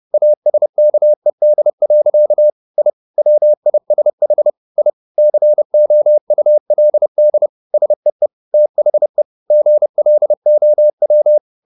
Morse code, heard as a keyed tone, about 30 wpm, 600 Hz. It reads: ASKED. I WISH I COULD SEE THE GLOW